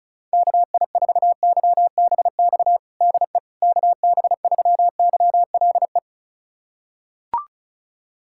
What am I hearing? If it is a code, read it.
Morse code, heard as a keyed tone, about 36 wpm, 700 Hz. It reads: KI4YBX DE KB3YLE
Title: USA Callsign DE Callsign - 6 Characters - 35wpm